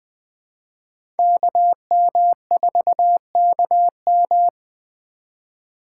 Morse code, heard as a keyed tone, about 20 wpm, 700 Hz. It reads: KM4KM